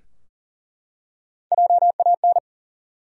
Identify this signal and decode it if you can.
Morse code, heard as a keyed tone, about 37 wpm, 700 Hz. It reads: JAN
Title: USA Common Names - Max 5 Letters - 40wpm